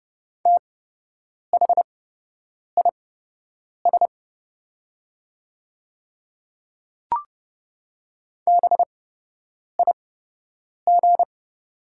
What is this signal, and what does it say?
Morse code, heard as a keyed tone, about 30 wpm, spaced as for 8 wpm, 690 Hz. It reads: THIS BIG
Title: 8x Character Spacing - Top 200 Words - 30wpm